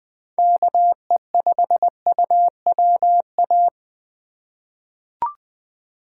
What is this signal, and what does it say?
Morse code, heard as a keyed tone, about 20 wpm, 700 Hz. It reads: KE5UWA
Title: USA Call Signs - 6 Characters - 2x Character Spacing - 20wpm